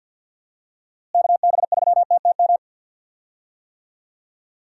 Morse code, heard as a keyed tone, about 52 wpm, 700 Hz. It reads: KB3TTM
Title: USA Call Signs - 6 Characters - Repeated 3x - 50wpm